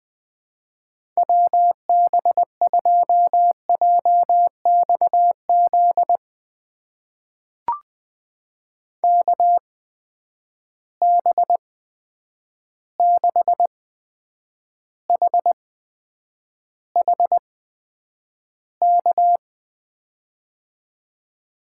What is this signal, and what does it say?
Morse code, heard as a keyed tone, about 21 wpm, spaced as for 9 wpm, 700 Hz. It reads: WB2JXZ KB6HHK
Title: USA Call Signs - 6 Characters - 8x Character Spacing - 20wpm